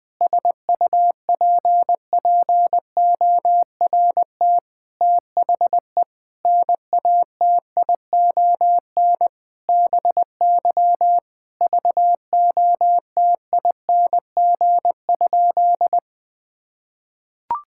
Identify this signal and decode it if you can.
Morse code, heard as a keyed tone, about 21 wpm, 700 Hz. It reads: SUPPORT THE NATION BY VOTING?
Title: Sentences with 9-Word Phrases - 20wpm